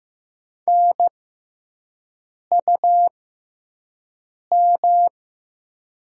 Morse code, heard as a keyed tone, about 15 wpm, spaced as for 5 wpm, 700 Hz. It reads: NUM
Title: 6x Character Spacing - Top 100 Words - 15wpm